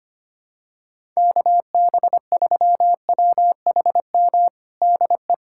Morse code, heard as a keyed tone, about 25 wpm, 700 Hz. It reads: KB3WHM DE
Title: USA Callsign DE Callsign - 6 Characters - 25wpm